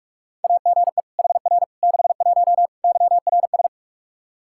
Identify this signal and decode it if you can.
Morse code, heard as a keyed tone, about 42 wpm, 700 Hz. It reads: AGE HR 61 YRS
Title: Practice QSO - Exchange 3 - More Info - 45wpm